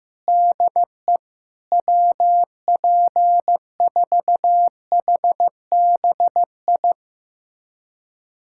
Morse code, heard as a keyed tone, about 15 wpm, 700 Hz. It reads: DE WP4HBI
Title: USA Callsign DE Callsign - 6 Characters - 15wpm